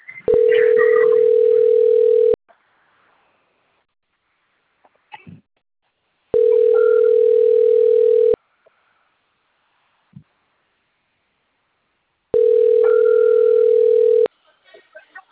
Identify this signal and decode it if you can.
Ringback tone (North America), 440 + 480 Hz, with speech.
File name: rg-1600-unknown